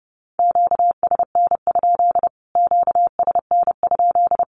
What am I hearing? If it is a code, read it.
Morse code, heard as a keyed tone, about 30 wpm, 700 Hz. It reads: QSN? QSN?
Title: Q-Codes Repeated 3x 30wpm